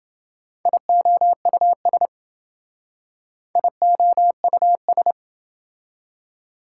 Morse code, heard as a keyed tone, about 28 wpm, 700 Hz. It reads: IOUS IOUS